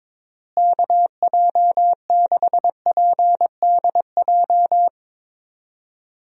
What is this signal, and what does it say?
Morse code, heard as a keyed tone, about 22 wpm, 700 Hz. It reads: KJ6PDJ
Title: USA Call Signs - 6 Characters - Repeated 2x - 22wpm